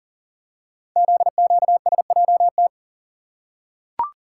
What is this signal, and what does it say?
Morse code, heard as a keyed tone, about 40 wpm, 700 Hz. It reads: ZQSJT
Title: Code Groups - 5 Letters - 40wpm